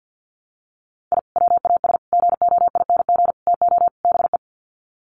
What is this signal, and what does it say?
Morse code, heard as a keyed tone, about 50 wpm, 700 Hz. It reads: I WAS GOING TO BE
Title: Sets of 5 Words 50wpm